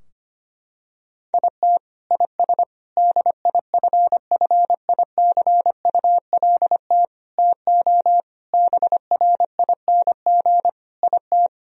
Morse code, heard as a keyed tone, about 25 wpm, 700 Hz. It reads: IT IS DIFFICULT TO BRING IT
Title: Sentences from Top 1000 Words - Review Part 4 - 25wpm